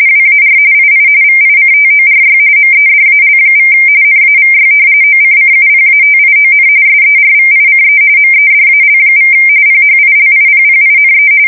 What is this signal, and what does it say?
RTTY, about 45.45 baud. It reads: WAR II VICTORY MEDAL= NAVY OCCUPATION SERVICE MEDAL= PHILIPPINE
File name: fsk170 / battleshipnj